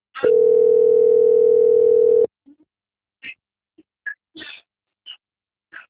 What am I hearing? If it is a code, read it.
Ringback tone (North America), 440 + 480 Hz.
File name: out- - - - - .37836